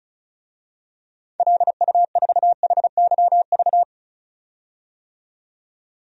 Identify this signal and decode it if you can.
Morse code, heard as a keyed tone, about 34 wpm, 700 Hz. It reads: LU4HYV
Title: DXCC-100 Call Signs - 6 Characters - 35wpm